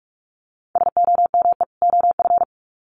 Morse code, heard as a keyed tone, about 44 wpm, 700 Hz. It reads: SOME OF
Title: Sets of 2 Words 45wpm